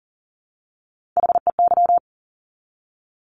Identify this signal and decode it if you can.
Morse code, heard as a keyed tone, about 40 wpm, 700 Hz. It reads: HEY
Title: Top Three Letter Combinations in Words 40wpm